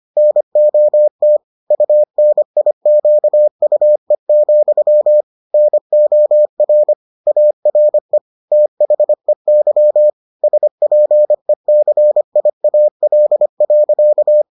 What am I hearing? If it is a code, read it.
Morse code, heard as a keyed tone, about 25 wpm, 600 Hz. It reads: NOT UNIQUE, NOR ARE THEY SPECIAL.